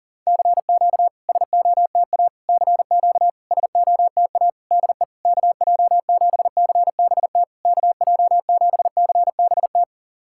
Morse code, heard as a keyed tone, about 40 wpm, 660 Hz. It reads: CQ SOTA CQ SOTA DE KJ7CBT KJ7CBT
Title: SOTA QSO Practice 40wpm